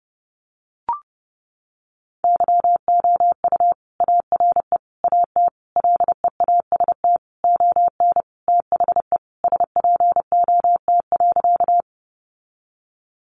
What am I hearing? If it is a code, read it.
Morse code, heard as a keyed tone, about 30 wpm, 700 Hz. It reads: YOU ARE AT LEAST ON THE SPOT.
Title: Sentences from Top 1100 Words 30wpm